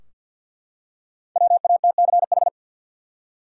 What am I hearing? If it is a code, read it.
Morse code, heard as a keyed tone, about 53 wpm, 700 Hz. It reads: WATCH
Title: Five Letter Words - 50wpm